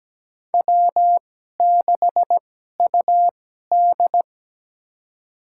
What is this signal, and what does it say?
Morse code, heard as a keyed tone, about 17 wpm, spaced as for 12 wpm, 700 Hz. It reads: W6UD
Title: USA Call Signs - 4 Characters - 2x Character Spacing - 17wpm